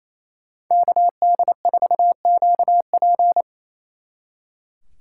Morse code, heard as a keyed tone, about 37 wpm, 700 Hz. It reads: KD4QP